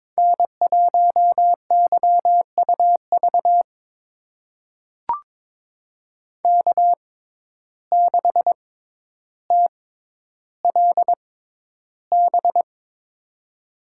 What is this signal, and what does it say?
Morse code, heard as a keyed tone, about 22 wpm, spaced as for 11 wpm, 700 Hz. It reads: N1YUV K6TLB
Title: USA Call Signs - 5 Characters - 6x Character Spacing - 22wpm